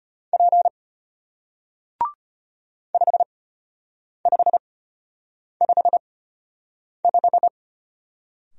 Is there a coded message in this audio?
Yes: Morse code from 0.3 to 7.5 s, heard as a keyed tone, about 33 wpm, spaced as for 8 wpm, 700 Hz. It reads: P 5555